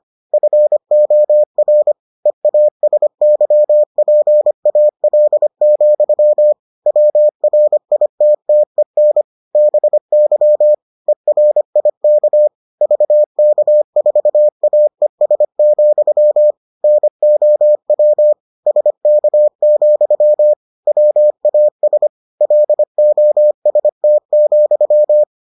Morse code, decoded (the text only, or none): FOR EASYPAL, WRITTEN BY ERIK VK4AES, NOW SK, WAS LOST,